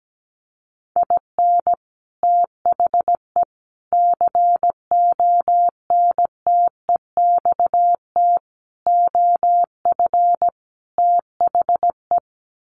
Morse code, heard as a keyed tone, about 17 wpm, 700 Hz. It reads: IN THE CONTEXT OF THE
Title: Sets of 5 Words 17wpm